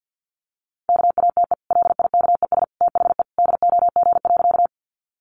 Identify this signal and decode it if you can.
Morse code, heard as a keyed tone, about 50 wpm, 700 Hz. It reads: KATE LIKES THE DOG.
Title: Easy Phrases 50wpm